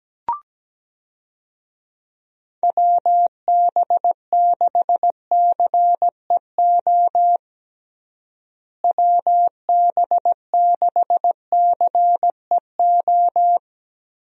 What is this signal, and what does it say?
Morse code, heard as a keyed tone, about 17 wpm, 700 Hz. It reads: WB6CEO WB6CEO
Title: USA Call Signs - 6 Characters - Repeated 3x - 17wpm